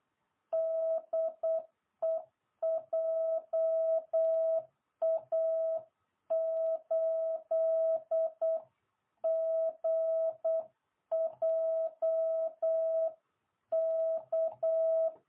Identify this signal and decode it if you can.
Morse code, heard as a keyed tone, about 8 wpm, 660 Hz. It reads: DE JA8GJK